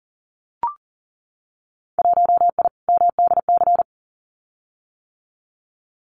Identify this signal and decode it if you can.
Morse code, heard as a keyed tone, about 42 wpm, 700 Hz. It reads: 1I MDC
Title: Winter Field Day Exchanges 40wpm